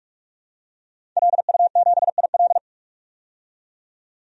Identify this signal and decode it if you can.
Morse code, heard as a keyed tone, about 45 wpm, 700 Hz. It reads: LU7IL
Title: DXCC-100 Call Signs - 5 Characters - 45wpm